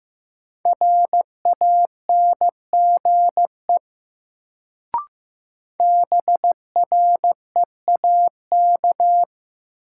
Morse code, heard as a keyed tone, about 15 wpm, 700 Hz. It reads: RANGE BREAK
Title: Sentences from Top 900 Words - New Word Drill - 15wpm